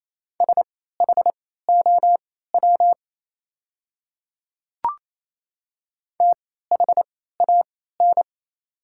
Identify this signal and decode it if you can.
Morse code, heard as a keyed tone, about 28 wpm, spaced as for 16 wpm, 700 Hz. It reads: SHOW THAN
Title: 3x Character Spacing - Top 200 Words - 28wpm